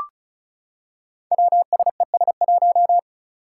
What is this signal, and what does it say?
Morse code, heard as a keyed tone, about 34 wpm, 700 Hz. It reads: WSES1
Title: Five Challenging Letter-Number Combinations 35wpm